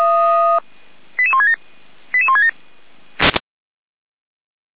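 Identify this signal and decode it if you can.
Five-tone selective call (ZVEI), heard as quick runs of five tones, tones of about 80 ms. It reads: 88167 88167
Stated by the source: ZVEI five-tone call received by SDR, recorded by Feuerwehr Orlamünde.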